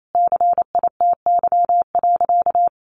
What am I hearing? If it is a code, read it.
Morse code, heard as a keyed tone, about 28 wpm, 700 Hz. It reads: CITY.
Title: Sentences from Top 1500 Words 28wpm